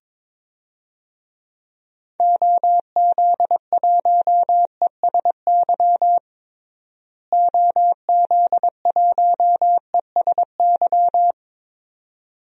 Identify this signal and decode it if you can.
Morse code, heard as a keyed tone, about 22 wpm, 700 Hz. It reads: OZ1ESY OZ1ESY
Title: DXCC-100 Call Signs - 6 Characters - Repeated 2x - 22wpm